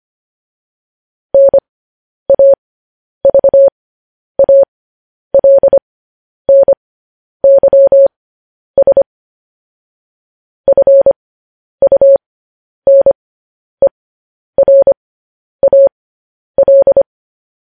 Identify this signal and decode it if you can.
Morse code, heard as a keyed tone, about 25 wpm, spaced as for 10 wpm, 550 Hz. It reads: NAVALNYS FUNERAL